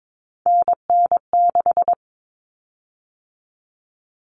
Morse code, heard as a keyed tone, about 22 wpm, 700 Hz. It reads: NN6
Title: Common Cut Numbers - 3-Digit 22wpm